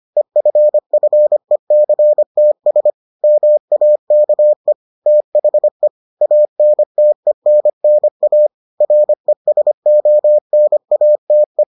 Morse code, heard as a keyed tone, about 25 wpm, 600 Hz. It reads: EFFECTS MAKE THE ANTENNA RESONATE